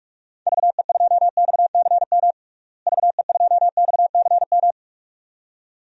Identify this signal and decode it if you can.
Morse code, heard as a keyed tone, about 42 wpm, 700 Hz. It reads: VE2XCM VE2XCM
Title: DXCC-100 Call Signs - 6 Characters - Repeated 2x - 45wpm